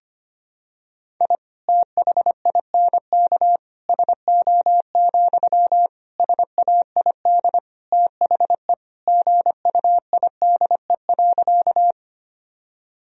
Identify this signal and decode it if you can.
Morse code, heard as a keyed tone, about 25 wpm, 700 Hz. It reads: I THINK SO, SAID THE GUIDE.
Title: Sentences from Top 1000 Words - Review Part 3 - 25wpm